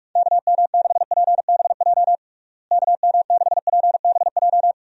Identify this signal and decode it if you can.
Morse code, heard as a keyed tone, about 42 wpm, 700 Hz. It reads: KM6PBJ KM6PBJ
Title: USA Call Signs - 6 Characters - Repeated 2x - 45wpm